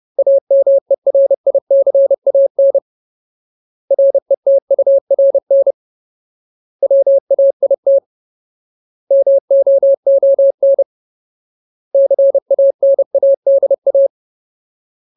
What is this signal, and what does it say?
Morse code, heard as a keyed tone, about 30 wpm, 550 Hz. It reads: AMERICAN RETURN WAIT MOON CANADA
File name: Ultimate Word List / 30WPM Ultimate